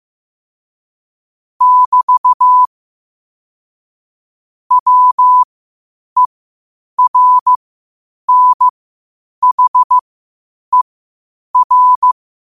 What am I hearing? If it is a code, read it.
Morse code, heard as a keyed tone, about 15 wpm, spaced as for 8 wpm, 1000 Hz. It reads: <BT> WERNHER
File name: qotdfiles / 8wpm